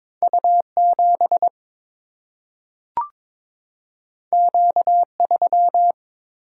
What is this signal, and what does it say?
Morse code, heard as a keyed tone, about 22 wpm, 700 Hz. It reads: U 7 Q 3
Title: Single-Letter Single-Number 22wpm